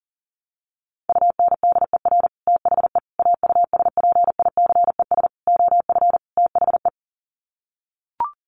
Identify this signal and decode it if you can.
Morse code, heard as a keyed tone, about 40 wpm, 700 Hz. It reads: UNDER THE AUSPICES OF THE
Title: Sets of 5 Words Encore2 40wpm